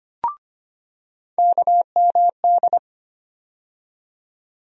Morse code, heard as a keyed tone, about 25 wpm, 700 Hz. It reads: KMD
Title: USA Call Sign Postfixes - 3 Letters 25wpm